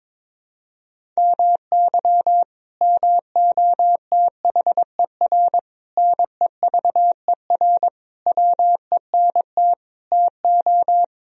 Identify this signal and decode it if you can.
Morse code, heard as a keyed tone, about 22 wpm, 700 Hz. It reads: MY MOTHER NEVER WENT TO
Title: Sentences from Top 1000 Words - Review Part 1 - 22wpm